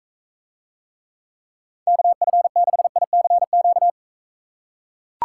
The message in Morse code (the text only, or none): KF6ICQ